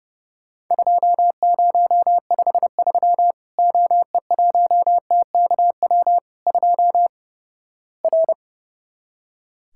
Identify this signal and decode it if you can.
Morse code, heard as a keyed tone, about 29 wpm, 700 Hz. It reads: 2053 OE1TKW 2 R
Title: Worked All Europe DX Contest - QTC Exchange - 30wpm